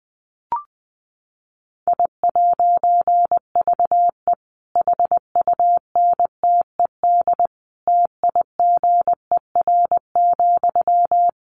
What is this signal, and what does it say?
Morse code, heard as a keyed tone, about 20 wpm, 700 Hz. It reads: I'VE HUNTED TIGER,
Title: A Sound Of Thunder 20wpm